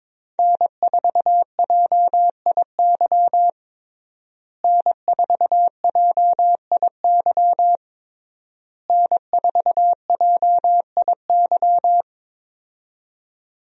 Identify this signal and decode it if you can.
Morse code, heard as a keyed tone, about 23 wpm, 700 Hz. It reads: N4JIY N4JIY N4JIY
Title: USA Call Signs - 5 Characters - Repeated 3x - 22wpm